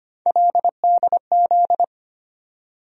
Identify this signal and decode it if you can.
Morse code, heard as a keyed tone, about 25 wpm, 700 Hz. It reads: LDZ
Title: USA Call Sign Postfixes - 3 Letters 25wpm